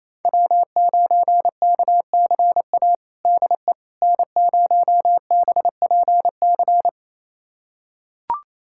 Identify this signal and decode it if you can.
Morse code, heard as a keyed tone, about 28 wpm, 700 Hz. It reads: W9KCA DE N0BPC
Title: USA Callsign DE Callsign - 5 Characters - 28wpm